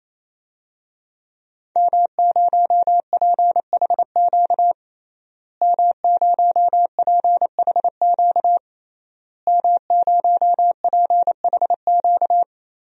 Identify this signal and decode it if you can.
Morse code, heard as a keyed tone, about 28 wpm, 700 Hz. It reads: M0PHQ M0PHQ M0PHQ
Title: DXCC-100 Call Signs - 5 Characters - Repeated 3x - 28wpm